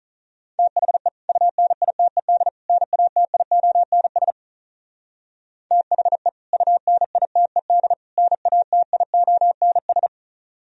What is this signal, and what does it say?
Morse code, heard as a keyed tone, about 43 wpm, 690 Hz. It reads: THE UNITED NATIONS THE UNITED NATIONS
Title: Speed-Racing - Sets of 3 Words 35wpm